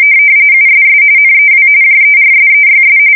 RTTY, about 45.45 baud. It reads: USS NEW JERSEYS
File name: fsk170 / battleshipnj